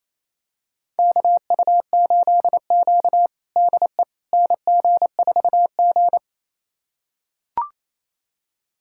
Morse code, heard as a keyed tone, about 27 wpm, 700 Hz. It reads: KU8Q DE NG4G
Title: USA Callsign DE Callsign - 4 Characters - 28wpm